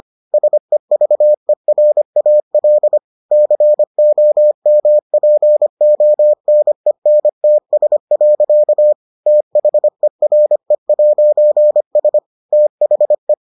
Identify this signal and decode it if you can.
Morse code, heard as a keyed tone, about 25 wpm, 600 Hz. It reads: SEVERAL COMPONENTS. THERE'S THE